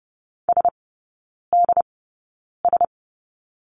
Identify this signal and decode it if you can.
Morse code, heard as a keyed tone, about 30 wpm, spaced as for 9 wpm, 700 Hz. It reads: SDS
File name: Lesson 19e - Letter w - Three Characters 30wpm